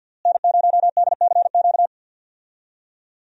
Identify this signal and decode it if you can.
Morse code, heard as a keyed tone, about 52 wpm, 690 Hz. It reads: N0DCQ
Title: USA Call Signs - 5 Characters - Repeated 3x - 50wpm